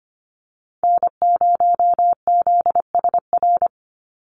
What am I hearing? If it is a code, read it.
Morse code, heard as a keyed tone, about 25 wpm, 700 Hz. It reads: N0ZSR
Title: USA Call Signs 25wpm